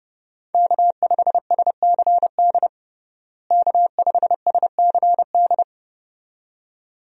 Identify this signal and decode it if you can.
Morse code, heard as a keyed tone, about 32 wpm, 700 Hz. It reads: K5SCD K5SCD
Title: USA Call Signs - 5 Characters - Repeated 2x - 30wpm